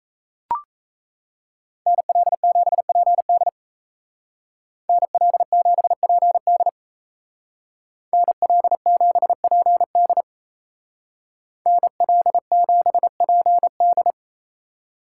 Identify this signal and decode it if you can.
Morse code, heard as a keyed tone, about 34 wpm, 700 Hz. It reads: NL7PD NL7PD NL7PD NL7PD
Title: Speed Racing - USA Call Signs 28wpm